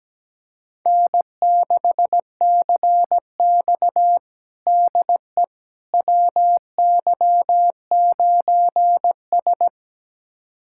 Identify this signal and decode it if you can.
Morse code, heard as a keyed tone, about 17 wpm, 700 Hz. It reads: N6CX DE WY9S
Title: USA Callsign DE Callsign - 4 Characters - 17wpm